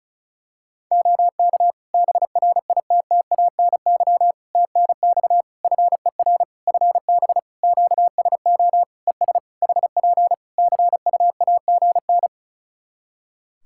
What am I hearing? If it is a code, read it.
Morse code, heard as a keyed tone, about 36 wpm, 700 Hz. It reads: OK BRITTANY TNX FER FB QSO ES HP CUAGN
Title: K1USN SST QSO Practice 35wpm